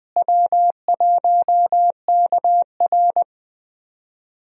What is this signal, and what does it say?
Morse code, heard as a keyed tone, about 20 wpm, 700 Hz. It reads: W1KR
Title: USA Call Signs - 4 Characters - Repeated 3x - 20wpm